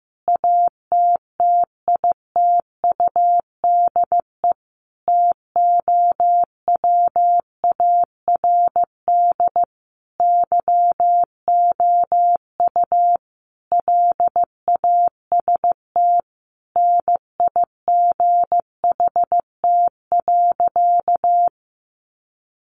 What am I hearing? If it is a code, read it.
Morse code, heard as a keyed tone, about 15 wpm, 700 Hz. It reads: ATTITUDE TOWARD YOU LAST NIGHT.
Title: Sentences from Top 1300 Words 15wpm